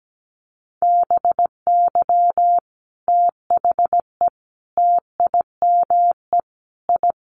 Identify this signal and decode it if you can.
Morse code, heard as a keyed tone, about 17 wpm, 700 Hz. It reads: BY THE TIME I
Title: Sets of 5 Words Encore2 17wpm